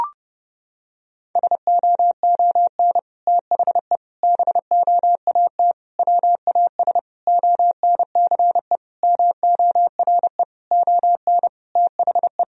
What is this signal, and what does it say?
Morse code, heard as a keyed tone, about 30 wpm, 700 Hz. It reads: SOON THE BOAT WAS ONCE MORE ON THE
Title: Sentences from Top 1000 Words - Review Part 3 - 30wpm